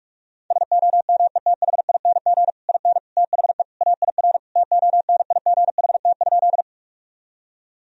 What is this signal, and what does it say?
Morse code, heard as a keyed tone, about 41 wpm, 700 Hz. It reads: SOMETHING IN THE AIR TONIGHT?
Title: Sentences from Top 1000 Words - Review Part 1 - 45wpm